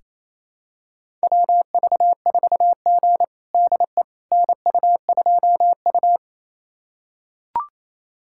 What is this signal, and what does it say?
Morse code, heard as a keyed tone, about 28 wpm, 700 Hz. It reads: WV4G DE NU2U
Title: USA Callsign DE Callsign - 4 Characters - 28wpm